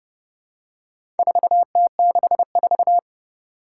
Morse code, heard as a keyed tone, about 29 wpm, 700 Hz. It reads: 4T64
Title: Four Challenging Letter-Number Combinations 30wpm